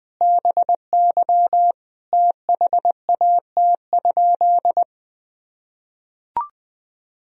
Morse code, heard as a keyed tone, about 19 wpm, 700 Hz. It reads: BY THAT?
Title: Sentences from Top 200 Words and Abbreviations 20wpm